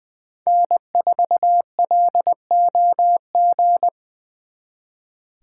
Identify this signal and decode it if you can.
Morse code, heard as a keyed tone, about 25 wpm, 700 Hz. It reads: N4LOG
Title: Speed Racing - USA 1x3 Call Signs 20wpm